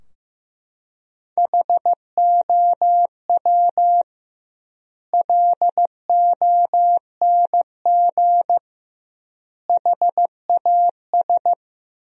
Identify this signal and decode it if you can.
Morse code, heard as a keyed tone, about 15 wpm, 700 Hz. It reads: HOW LONG HAS
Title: 2x Word-Spacing - Sentences from Top 100 Words - 15wpm